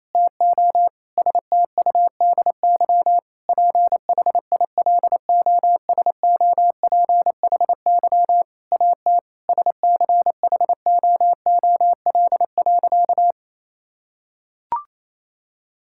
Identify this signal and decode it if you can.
Morse code, heard as a keyed tone, about 28 wpm, 700 Hz. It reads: TO STUDY PHILOSOPHY AT SCHOOL.
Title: Sentences from Top 1000 Words - Review Part 6 - 28wpm